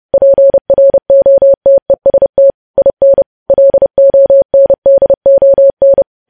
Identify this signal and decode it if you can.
Morse code, heard as a keyed tone, about 30 wpm, 550 Hz. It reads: PROTEST IN LONDON
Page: news headlines in morse code at 30 wpm